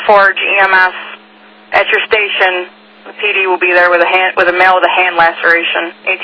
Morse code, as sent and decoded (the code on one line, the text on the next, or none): none